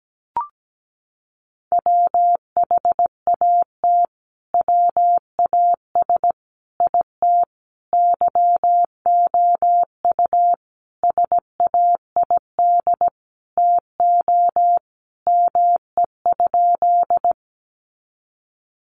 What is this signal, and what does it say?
Morse code, heard as a keyed tone, about 17 wpm, 700 Hz. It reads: WHAT WAS IT YOU SAID TO ME?
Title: Sentences from Top 200 Words 17wpm